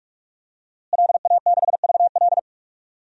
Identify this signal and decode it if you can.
Morse code, heard as a keyed tone, about 46 wpm, 700 Hz. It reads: LA6VL
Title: DXCC-100 Call Signs - 5 Characters - Repeated 3x - 45wpm